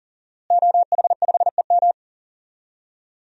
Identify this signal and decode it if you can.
Morse code, heard as a keyed tone, about 40 wpm, 700 Hz. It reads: OH5EM
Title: DXCC-100 Call Signs - 5 Characters - Repeated 3x - 40wpm